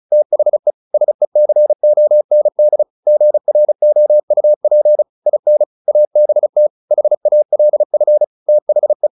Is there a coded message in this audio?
Yes: Morse code from 0.1 to 9.1 s, heard as a keyed tone, about 34 wpm, 600 Hz. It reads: THE SECOND GROUP IN ABT HALF THE